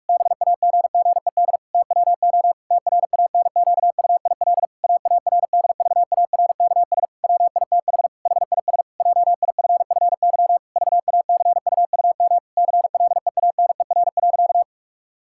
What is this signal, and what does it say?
Morse code, heard as a keyed tone, about 46 wpm, 700 Hz. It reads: BAGGED TWO TRANQUIL AARDVARKS WITH HIS JIFFY VACUUM CLEANER.
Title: Pangram Challenge v2 - 45wpm